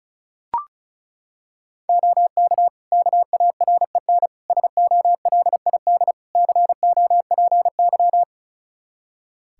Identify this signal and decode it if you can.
Morse code, heard as a keyed tone, about 36 wpm, 700 Hz. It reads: OK KAREN SOLID COPY
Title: Practice QSO - Exchange 3 - More Info - 35wpm